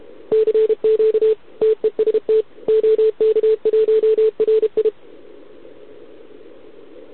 Morse code, heard as a keyed tone, about 32 wpm, 440 Hz. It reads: CQ TEST OK1RI
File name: ARCHIV / OKDX2020 / OK1RIb